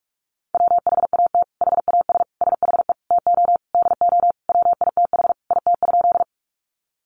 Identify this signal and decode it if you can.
Morse code, heard as a keyed tone, about 44 wpm, 700 Hz. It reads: WHAT HAS SHE TO DO WITH IT?
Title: Sentences from Top 100 Words 45wpm